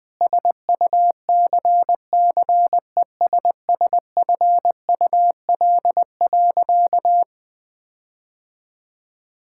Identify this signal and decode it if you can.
Morse code, heard as a keyed tone, about 19 wpm, 700 Hz. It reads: SUCCESSFUL.
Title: Sentences with 7-Word Phrases - 20wpm